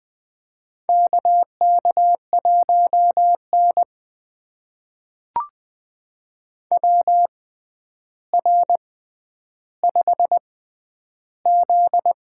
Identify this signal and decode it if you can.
Morse code, heard as a keyed tone, about 20 wpm, spaced as for 10 wpm, 700 Hz. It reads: KK1N WR5Z
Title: USA Call Signs - 4 Characters - 6x Character Spacing - 20wpm